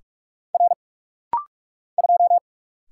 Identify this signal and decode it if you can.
Morse code, heard as a keyed tone, about 44 wpm, 700 Hz. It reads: R 2
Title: Single Letter-Number Alt - Rapid-Fire 45wpm